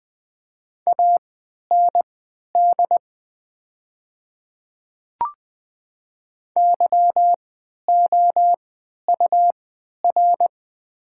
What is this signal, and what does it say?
Morse code, heard as a keyed tone, about 20 wpm, spaced as for 11 wpm, 700 Hz. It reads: AND YOUR
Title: 3x Character Spacing - Top 100 Words - 20wpm